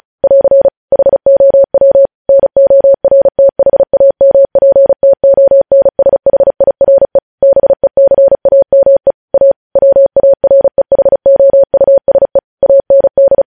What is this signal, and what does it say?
Morse code, heard as a keyed tone, about 36 wpm, 550 Hz. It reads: <AR> HOW NORTHAMPTONSHIRE BECAME A WAREHOUSE AND
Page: News Headlines in Morse Code at 35 WPM